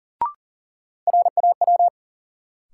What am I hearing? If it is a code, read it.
Morse code, heard as a keyed tone, about 40 wpm, 700 Hz. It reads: RAW
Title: Sentences from Top 1500 Words - New Word Drill Alt - 40wpm